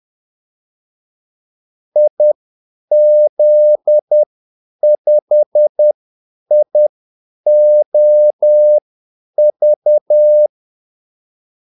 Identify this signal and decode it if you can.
Morse code, heard as a keyed tone, about 10 wpm, spaced as for 8 wpm, 600 Hz. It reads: IZ5IOV
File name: Callsigns-10-8wpm